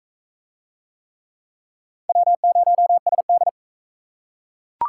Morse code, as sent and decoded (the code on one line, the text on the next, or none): .-- ----- ... -..
W0SD